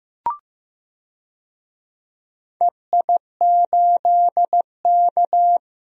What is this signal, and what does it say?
Morse code, heard as a keyed tone, about 15 wpm, 700 Hz. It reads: EI8K
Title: DXCC-100 Call Signs - 4 Characters - Repeated 3x - 15wpm